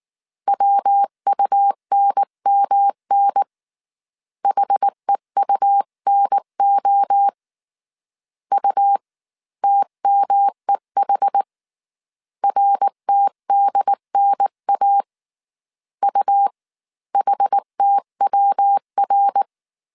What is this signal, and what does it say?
Morse code, heard as a keyed tone, about 19 wpm, spaced as for 16 wpm, 800 Hz. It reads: WUNMN HEUNO UTMEH RTDNA UHTWR